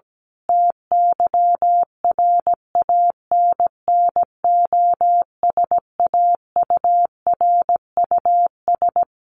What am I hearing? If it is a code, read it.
Morse code, heard as a keyed tone, about 17 wpm, 700 Hz. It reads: TYRANNOSAURUS
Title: A Sound Of Thunder 17wpm